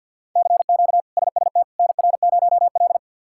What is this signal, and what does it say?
Morse code, heard as a keyed tone, about 48 wpm, 700 Hz. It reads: CQ SST NR0L
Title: K1USN SST QSO Practice - 50wpm